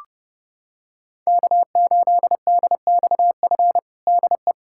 Morse code, heard as a keyed tone, about 29 wpm, 700 Hz. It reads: K8DXF DE
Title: USA Callsign DE Callsign - 5 Characters - 30wpm